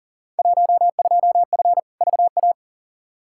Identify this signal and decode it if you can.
Morse code, heard as a keyed tone, about 39 wpm, 700 Hz. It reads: 12F VA
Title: Field Day Exchanges Encore - 40wpm